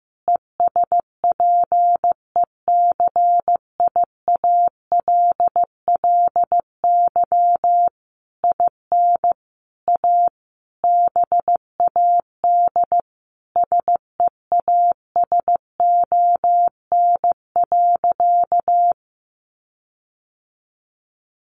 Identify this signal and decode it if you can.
Morse code, heard as a keyed tone, about 14 wpm, 700 Hz. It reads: ESPECIALLY IN A BAD SEASON.
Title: Sentences from Top 1400 Words 15wpm